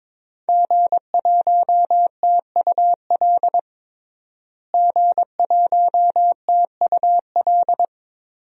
Morse code, heard as a keyed tone, about 22 wpm, 700 Hz. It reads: G1TUL G1TUL
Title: DXCC-100 Call Signs - 5 Characters - Repeated 3x - 22wpm